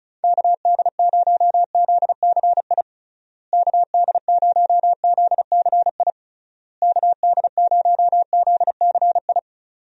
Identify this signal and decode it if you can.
Morse code, heard as a keyed tone, about 34 wpm, 700 Hz. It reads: KD0ZCI KD0ZCI KD0ZCI
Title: USA Call Signs - 6 Characters - Repeated 3x - 35wpm